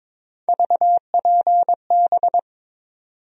Morse code, heard as a keyed tone, about 22 wpm, 700 Hz. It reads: VPB
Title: USA Call Sign Postfixes - 3 Letters 22wpm